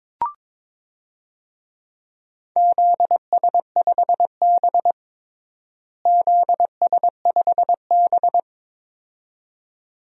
Morse code, heard as a keyed tone, about 22 wpm, 700 Hz. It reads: ZS5B ZS5B
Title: DXCC-100 Call Signs - 4 Characters - Repeated 2x - 22wpm